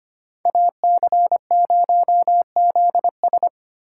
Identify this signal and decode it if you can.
Morse code, heard as a keyed tone, about 28 wpm, 700 Hz. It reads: AC0ZS
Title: Speed Racing - USA 2x2 Call Signs 25wpm